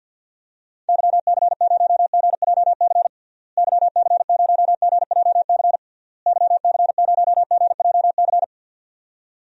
Morse code, heard as a keyed tone, about 50 wpm, 700 Hz. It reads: YC0GJC YC0GJC YC0GJC
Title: DXCC-100 Call Signs - 6 Characters - Repeated 3x - 50wpm